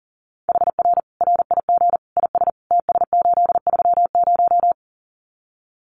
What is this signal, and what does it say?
Morse code, heard as a keyed tone, about 40 wpm, 700 Hz. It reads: HR RIG IS TS830
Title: Common QSO Phrases 40wpm